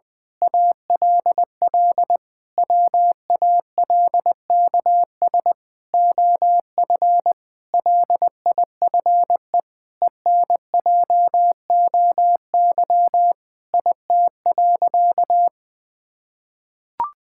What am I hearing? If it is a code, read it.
Morse code, heard as a keyed tone, about 20 wpm, 700 Hz. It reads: ALL WALKS OF LIFE ENJOY IT.
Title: Sentences with 6-Word Phrases - 20wpm